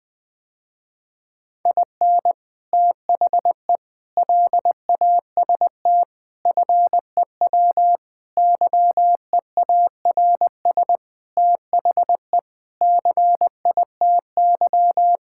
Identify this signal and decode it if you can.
Morse code, heard as a keyed tone, about 20 wpm, 700 Hz. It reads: IN THE LAST FEW YEARS THE CITY
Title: Sentences with 5-Word Phrases - Spoken Beginning - 20wpm